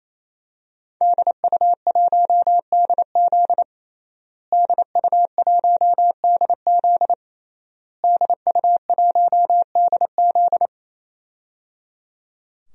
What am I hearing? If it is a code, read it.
Morse code, heard as a keyed tone, about 28 wpm, 700 Hz. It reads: DU1DZ DU1DZ DU1DZ
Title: DXCC-100 Call Signs - 5 Characters - Repeated 3x - 28wpm